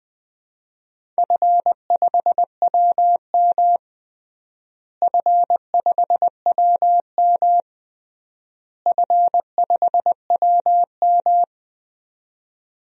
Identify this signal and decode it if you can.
Morse code, heard as a keyed tone, about 20 wpm, 700 Hz. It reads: F5WM F5WM F5WM
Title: DXCC-100 Call Signs - 4 Characters - Repeated 3x - 20wpm